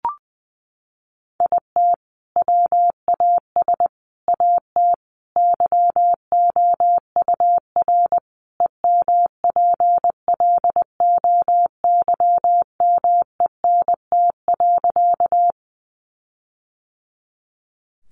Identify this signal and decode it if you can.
Morse code, heard as a keyed tone, about 20 wpm, 700 Hz. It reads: IT WAS AT YOUR EMPLOYMENT.
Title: Sentences from Top 1500 Words 20wpm